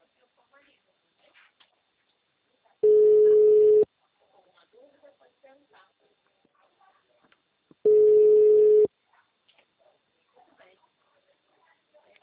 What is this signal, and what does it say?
Ringback tone (Europe, 425 Hz), with speech.